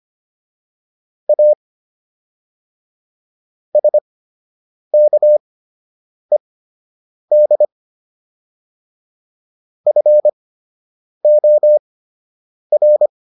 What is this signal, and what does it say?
Morse code, heard as a keyed tone, about 25 wpm, spaced as for 8 wpm, 600 Hz. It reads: A SKED FOR